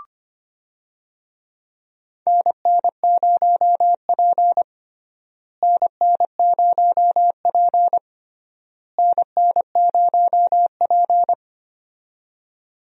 Morse code, heard as a keyed tone, about 25 wpm, 700 Hz. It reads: NN0P NN0P NN0P
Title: USA Call Signs - 4 Characters - Repeated 3x - 25wpm